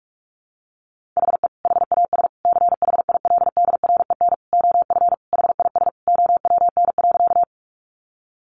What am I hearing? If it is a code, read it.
Morse code, heard as a keyed tone, about 44 wpm, 700 Hz. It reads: HE HAS CHILDREN OF HIS OWN.
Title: Sentences from Top 300 Words 45wpm